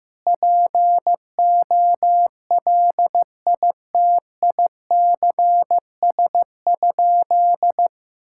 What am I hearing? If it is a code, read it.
Morse code, heard as a keyed tone, about 15 wpm, 700 Hz. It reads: POLITICS?
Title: Sentences from Top 1000 Words - Review Part 6 - 15wpm